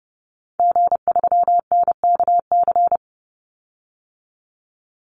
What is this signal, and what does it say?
Morse code, heard as a keyed tone, about 30 wpm, 700 Hz. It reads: G3NKC
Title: Call Signs Repeated 2x - 30wpm